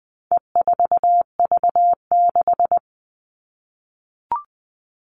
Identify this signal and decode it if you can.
Morse code, heard as a keyed tone, about 20 wpm, 700 Hz. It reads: E4V6
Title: All Cut Numbers - 4-Digit 20wpm